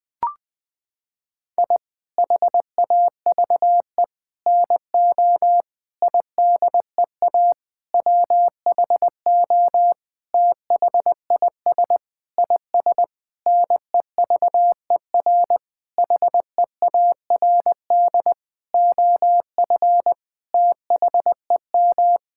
Morse code, heard as a keyed tone, about 21 wpm, 700 Hz. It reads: I HAVE NO IDEA WHO THIS IS NEVER HEARD OF THEM
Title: Sentences with 9-Word Phrases - 20wpm